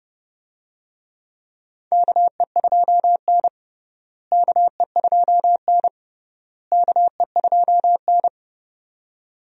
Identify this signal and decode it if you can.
Morse code, heard as a keyed tone, about 32 wpm, 700 Hz. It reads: KE2N KE2N KE2N
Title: USA Call Signs - 4 Characters - Repeated 3x - 30wpm